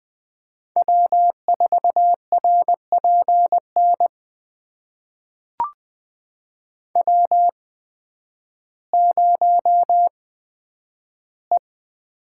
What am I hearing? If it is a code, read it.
Morse code, heard as a keyed tone, about 20 wpm, spaced as for 11 wpm, 700 Hz. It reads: W4RPN W0E
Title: USA Call Signs - 5 Characters - 8x Character Spacing - 20wpm